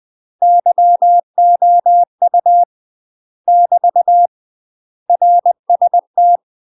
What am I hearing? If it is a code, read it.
Morse code, heard as a keyed tone, about 20 wpm, 700 Hz. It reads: YOU <BT> RST
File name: UK MASTER QSOs TRAINING / 20WPM / UK MASTER QSOs 0064